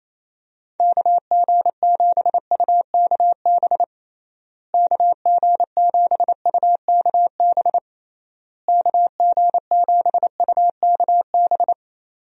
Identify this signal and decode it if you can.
Morse code, heard as a keyed tone, about 28 wpm, 700 Hz. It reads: KG7UKB KG7UKB KG7UKB
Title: USA Call Signs - 6 Characters - Repeated 3x - 28wpm